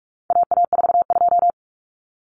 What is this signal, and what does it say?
Morse code, heard as a keyed tone, about 44 wpm, 700 Hz. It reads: AA42
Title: Common Cut Numbers - 4-Digit 45wpm